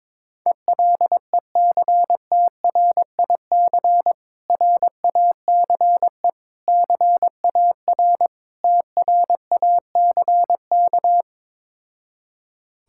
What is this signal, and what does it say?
Morse code, heard as a keyed tone, about 21 wpm, 700 Hz. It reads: ELECTRIC RACE CAR TRACK
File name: Difficult Words with Hints - 22wpm